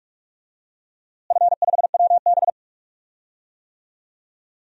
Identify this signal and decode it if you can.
Morse code, heard as a keyed tone, about 43 wpm, 700 Hz. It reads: F5WB
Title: DXCC-100 Call Signs - 4 Characters - Repeated 2x - 45wpm